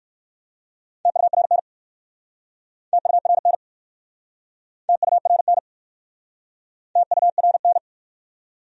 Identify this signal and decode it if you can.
Morse code, heard as a keyed tone, about 59 wpm, 700 Hz. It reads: TURN TURN TURN TURN